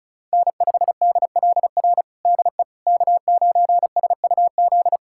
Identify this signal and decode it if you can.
Morse code, heard as a keyed tone, about 36 wpm, 700 Hz. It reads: N5DLR DE K9SUZ
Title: USA Callsign DE Callsign - 5 Characters - 35wpm